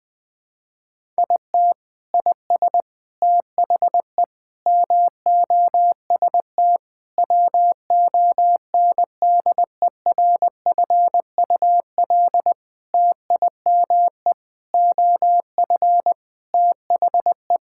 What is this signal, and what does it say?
Morse code, heard as a keyed tone, about 20 wpm, 700 Hz. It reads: IT IS THE MOST WONDERFUL TIME OF THE
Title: Sentences with 8-Word Phrases - 20wpm